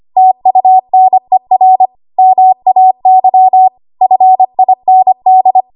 Morse code, heard as a keyed tone, about 25 wpm, 750 Hz. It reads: TUNER MAY FIND
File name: Archive / 25 WPM / 25WPM